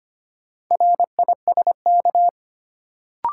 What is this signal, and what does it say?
Morse code, heard as a keyed tone, about 25 wpm, 700 Hz. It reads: RISK
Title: Sentences from Top 600 Words - New Word Drill - 25wpm